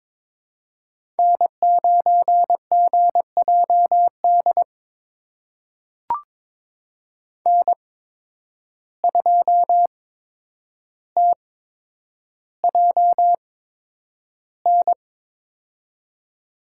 Morse code, heard as a keyed tone, about 22 wpm, spaced as for 9 wpm, 700 Hz. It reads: N9GJD N2TJN
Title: USA Call Signs - 5 Characters - 8x Character Spacing - 22wpm